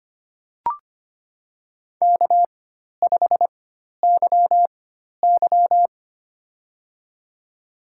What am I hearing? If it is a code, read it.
Morse code, heard as a keyed tone, about 25 wpm, spaced as for 12 wpm, 700 Hz. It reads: K5YY
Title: USA Call Signs - 4 Characters - 4x Character Spacing - 25wpm